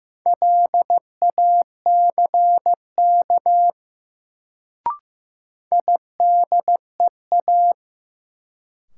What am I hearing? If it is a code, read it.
Morse code, heard as a keyed tone, about 15 wpm, 700 Hz. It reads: LACK IDEA
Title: Sets of 3 Words - New Word Drill - 15wpm